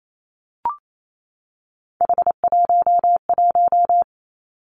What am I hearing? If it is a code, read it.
Morse code, heard as a keyed tone, about 28 wpm, 700 Hz. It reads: H11
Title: Three Challenging Letter-Number Combinations 28wpm